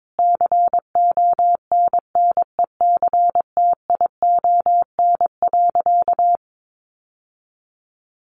Morse code, heard as a keyed tone, about 22 wpm, 700 Hz. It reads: CONNECTION.
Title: Sentences from Top 1300 Words 22wpm